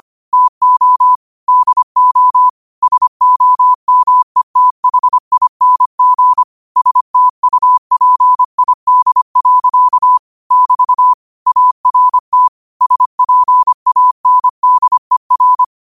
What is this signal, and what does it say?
Morse code, heard as a keyed tone, about 25 wpm, 1000 Hz. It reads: TO DO SOMETHING STUPID. <BT> ART SPANDER
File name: qotdfiles / 25wpm